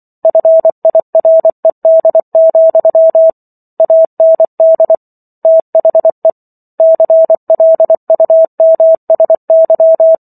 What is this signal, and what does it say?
Morse code, heard as a keyed tone, about 24 wpm, 650 Hz. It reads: FIRED, AND THE CLUMSY
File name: Chapter 15 - 24 WPM